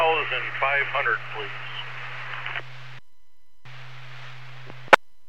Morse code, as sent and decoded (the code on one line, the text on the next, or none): none